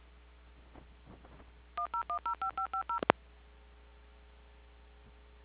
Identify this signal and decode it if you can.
DTMF, heard as telephone touch tones, tones of about 100 ms. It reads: 10105550